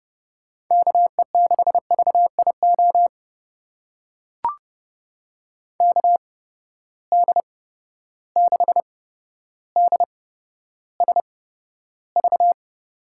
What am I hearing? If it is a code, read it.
Morse code, heard as a keyed tone, about 30 wpm, spaced as for 13 wpm, 700 Hz. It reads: KE6VIO KD6DSV
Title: USA Call Signs - 6 Characters - 8x Character Spacing - 30wpm